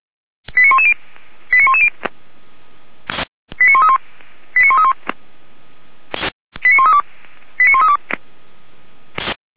Five-tone selective call (ZVEI), heard as quick runs of five tones, tones of about 70 ms. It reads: 89110 89110 89131 89131 89132 89132